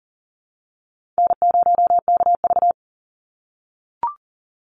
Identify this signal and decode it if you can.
Morse code, heard as a keyed tone, about 40 wpm, 700 Hz. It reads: N0KV
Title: Call Signs - 40wpm